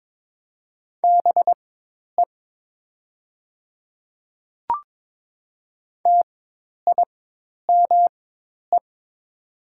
Morse code, heard as a keyed tone, about 22 wpm, spaced as for 10 wpm, 690 Hz. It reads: BE TIME
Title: 4x Character Spacing - Top 100 Words - 22wpm